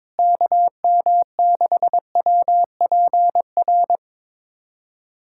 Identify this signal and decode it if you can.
Morse code, heard as a keyed tone, about 22 wpm, 700 Hz. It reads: KM6WPR
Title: USA Callsign DE Callsign 22wpm